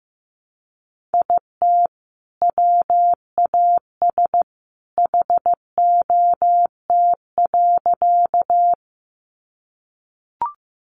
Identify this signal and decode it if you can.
Morse code, heard as a keyed tone, about 15 wpm, 700 Hz. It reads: IT WAS HOT.
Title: Sentences from Top 500 Words 15wpm